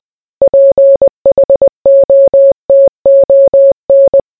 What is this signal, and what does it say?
Morse code, heard as a keyed tone, about 20 wpm, 550 Hz. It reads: PHOTON